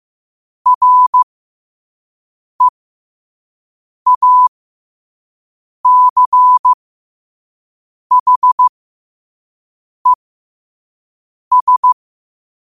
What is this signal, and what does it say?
Morse code, heard as a keyed tone, about 15 wpm, spaced as for 5 wpm, 1000 Hz. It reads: REACHES
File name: qotdfiles / 5wpm